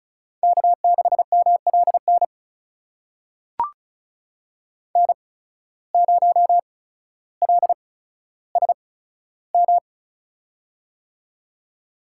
Morse code, heard as a keyed tone, about 36 wpm, spaced as for 15 wpm, 700 Hz. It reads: K6MLN N0LSM
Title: USA Call Signs - 5 Characters - 8x Character Spacing - 35wpm